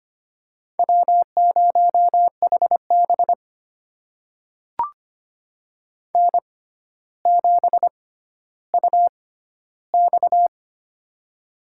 Morse code, heard as a keyed tone, about 25 wpm, spaced as for 13 wpm, 700 Hz. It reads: W0HB N7UX
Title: USA Call Signs - 4 Characters - 6x Character Spacing - 25wpm